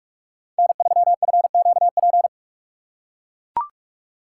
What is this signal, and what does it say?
Morse code, heard as a keyed tone, about 42 wpm, 700 Hz. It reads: N3FQP
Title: USA Call Signs - 5 Characters - 3x Character Spacing - 45wpm